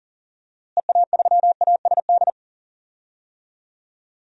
Morse code, heard as a keyed tone, about 37 wpm, 700 Hz. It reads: EA3ASD
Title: DXCC-100 Call Signs - 6 Characters - Repeated 3x - 40wpm